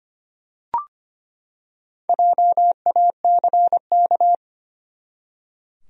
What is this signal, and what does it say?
Morse code, heard as a keyed tone, about 25 wpm, 700 Hz. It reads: JACK
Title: USA Common Names - Max 4 Letters - 25wpm